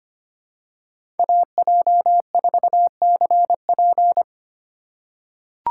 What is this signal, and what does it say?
Morse code, heard as a keyed tone, about 25 wpm, 700 Hz. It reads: AJ4CP
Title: USA Call Signs - 5 Characters - 2x Character Spacing - 25wpm